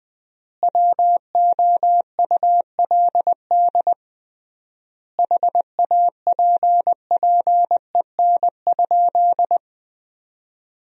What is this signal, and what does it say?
Morse code, heard as a keyed tone, about 21 wpm, 700 Hz. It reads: WOULD HAPPEN?
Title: 3x Word-Spacing - Sentences from Top 400 Words - 20wpm